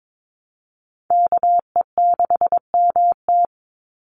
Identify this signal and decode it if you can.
Morse code, heard as a keyed tone, about 22 wpm, 700 Hz. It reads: KE6MT
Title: NA SOTA Chaser Callsigns 22wpm